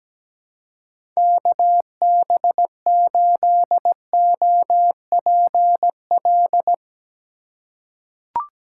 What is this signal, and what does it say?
Morse code, heard as a keyed tone, about 17 wpm, 700 Hz. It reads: KB8OPL
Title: USA Call Signs - 6 Characters - 4x Character Spacing - 17wpm